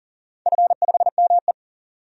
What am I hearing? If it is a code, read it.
Morse code, heard as a keyed tone, about 40 wpm, 700 Hz. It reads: F5ME
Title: DXCC-100 Call Signs - 4 Characters - 40wpm